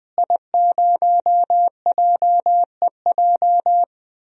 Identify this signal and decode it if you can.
Morse code, heard as a keyed tone, about 20 wpm, 700 Hz. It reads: I0JEJ
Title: DXCC-100 Call Signs - 5 Characters - Repeated 2x - 20wpm